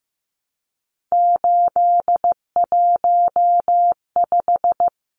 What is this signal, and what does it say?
Morse code, heard as a keyed tone, about 15 wpm, 700 Hz. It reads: 815
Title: Numbers - 3-Digit 15wpm